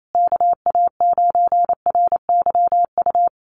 Morse code, heard as a keyed tone, about 28 wpm, 700 Hz. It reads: KA9RYU
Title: USA Call Signs Repeated 2x 28wpm